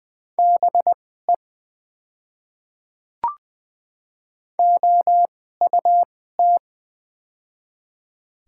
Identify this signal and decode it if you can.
Morse code, heard as a keyed tone, about 20 wpm, spaced as for 14 wpm, 700 Hz. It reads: BE OUT
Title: 2x Character Spacing - Top 100 Words - 20wpm